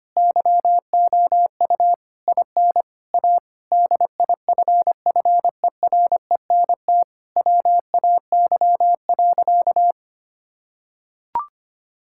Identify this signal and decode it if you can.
Morse code, heard as a keyed tone, about 25 wpm, 700 Hz. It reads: YOU IN A DIFFERENT WAY.
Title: Sentences from Top 1000 Words - Review Part 6 - 25wpm